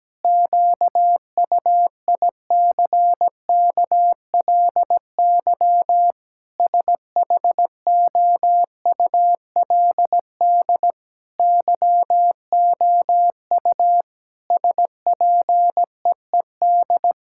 Morse code, heard as a keyed tone, about 17 wpm, 700 Hz. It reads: QUICKLY SHOULD YOU SPEED
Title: Sentences from Top 1000 Words - Review Part 5 - 17wpm